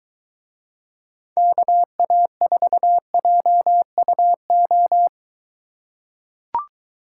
Morse code, heard as a keyed tone, about 23 wpm, 700 Hz. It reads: KA4JUO